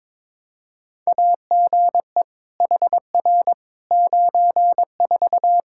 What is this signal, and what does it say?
Morse code, heard as a keyed tone, about 22 wpm, 700 Hz. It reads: AGE HR 94
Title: K1USN SST QSO Practice 22wpm